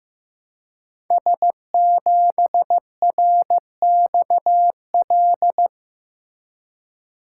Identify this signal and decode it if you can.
Morse code, heard as a keyed tone, about 15 wpm, 700 Hz. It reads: S7RXL
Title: Code Groups - 5 Letter:Numbers - 15wpm